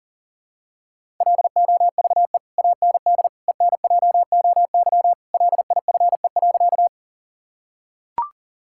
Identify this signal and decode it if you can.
Morse code, heard as a keyed tone, about 40 wpm, 700 Hz. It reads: LOVE AND ENJOY LIFE.
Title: Sentences from Top 1000 Words - Review Part 5 - 40wpm